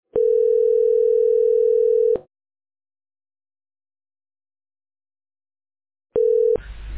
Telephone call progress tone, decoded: ringback tone